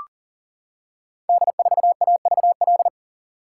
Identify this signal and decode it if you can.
Morse code, heard as a keyed tone, about 39 wpm, 700 Hz. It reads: D4AVL
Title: Code Groups - 5 Letter:Numbers - 40wpm